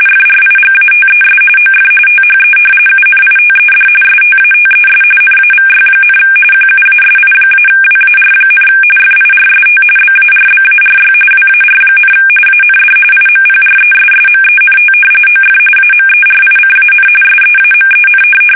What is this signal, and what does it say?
RTTY, about 45.45 baud. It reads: FRIGATES OF THE UNITED STATES NAVY, QUASI-WAR, FIRST BARBARY WAR, AND IMPRESSMENT OF AMERICAN SEAMEN
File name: fsk850 / usnavy1